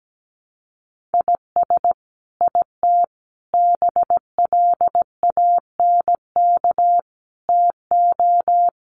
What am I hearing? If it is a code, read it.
Morse code, heard as a keyed tone, about 17 wpm, 700 Hz. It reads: IS IT BLANK TO
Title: Sentences from Top 1500 Words 17wpm